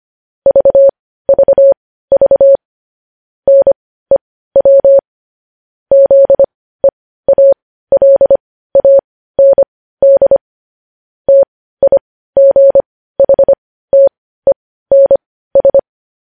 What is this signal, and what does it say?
Morse code, heard as a keyed tone, about 25 wpm, spaced as for 15 wpm, 550 Hz. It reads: VVV NEW ZEALAND TIGHTENS
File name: podcast / News headlines-1x-25x15